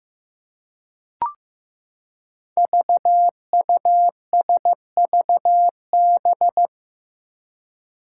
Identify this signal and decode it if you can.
Morse code, heard as a keyed tone, about 15 wpm, 700 Hz. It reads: VUSVB
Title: Five Challenging Letter-Number Combinations 15wpm